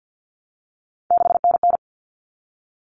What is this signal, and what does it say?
Morse code, heard as a keyed tone, about 50 wpm, 700 Hz. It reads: 6NN
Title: Common Cut Numbers - 3-Digit 50wpm